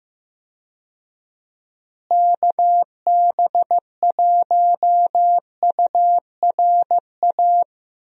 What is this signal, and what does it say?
Morse code, heard as a keyed tone, about 15 wpm, 700 Hz. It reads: KB1URA
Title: USA Call Signs - 6 Characters - Repeated 3x - 15wpm